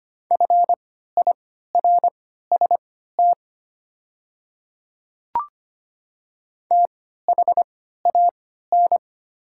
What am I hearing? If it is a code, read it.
Morse code, heard as a keyed tone, about 25 wpm, spaced as for 14 wpm, 700 Hz. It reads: FIRST THAN
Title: 3x Character Spacing - Top 100 Words - 25wpm